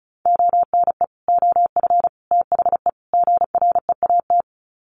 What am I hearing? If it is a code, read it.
Morse code, heard as a keyed tone, about 34 wpm, 700 Hz. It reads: ONE OF THE GREAT
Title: Sets of 4 Words 35wpm